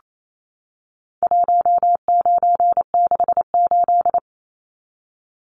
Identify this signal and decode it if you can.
Morse code, heard as a keyed tone, about 28 wpm, 700 Hz. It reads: 1968
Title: Numbers - 4-Digit 28wpm